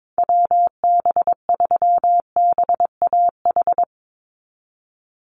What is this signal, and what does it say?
Morse code, heard as a keyed tone, about 22 wpm, 700 Hz. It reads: WB3BAH
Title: USA Call Signs 22wpm